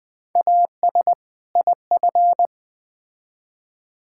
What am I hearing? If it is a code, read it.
Morse code, heard as a keyed tone, about 20 wpm, 700 Hz. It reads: AS IF
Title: Build A Phrase - 2-Word Phrases - 20wpm